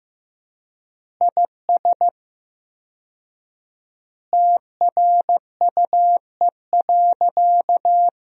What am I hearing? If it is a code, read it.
Morse code, heard as a keyed tone, about 15 wpm, 700 Hz. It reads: IS TRUE.
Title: 4x Word-Spacing - Sentences from Top 400 Words - 15wpm